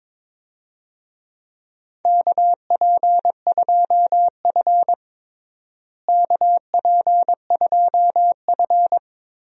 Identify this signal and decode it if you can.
Morse code, heard as a keyed tone, about 22 wpm, 700 Hz. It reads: KP2F KP2F
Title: DXCC-100 Call Signs - 4 Characters - Repeated 2x - 22wpm